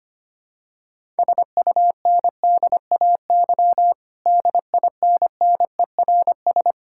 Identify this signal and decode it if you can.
Morse code, heard as a keyed tone, about 25 wpm, 700 Hz. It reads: SUNDAY DINNERS
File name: practice / 25wpm / Difficult Sentences with Hints - 25wpm